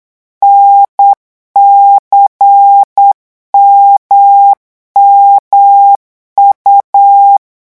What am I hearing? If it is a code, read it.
Morse code, heard as a keyed tone, about 8 wpm, 800 Hz. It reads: NCMMU
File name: morse mp3 / revision3-vitesse 8 mots